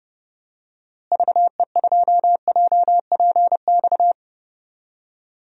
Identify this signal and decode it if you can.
Morse code, heard as a keyed tone, about 30 wpm, 700 Hz. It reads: VE2JPX
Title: DXCC-100 Call Signs - 6 Characters - Repeated 2x - 30wpm